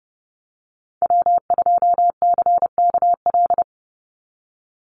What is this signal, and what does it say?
Morse code, heard as a keyed tone, about 32 wpm, 700 Hz. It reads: W2CKL
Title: NA SOTA Activator Callsigns 30wpm